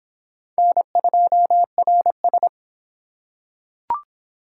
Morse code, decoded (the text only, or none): N2RS